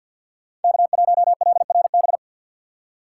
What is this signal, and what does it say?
Morse code, heard as a keyed tone, about 53 wpm, 700 Hz. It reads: K1LRB
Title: USA Call Signs - 5 Characters - 2x Character Spacing - 50wpm